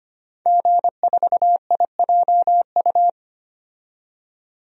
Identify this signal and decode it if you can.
Morse code, heard as a keyed tone, about 25 wpm, 700 Hz. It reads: G4IJU
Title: DXCC-100 Call Signs - 5 Characters - 25wpm